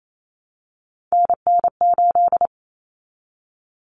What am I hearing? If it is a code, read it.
Morse code, heard as a keyed tone, about 28 wpm, 700 Hz. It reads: NN8
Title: Common Cut Numbers - 3-Digit 28wpm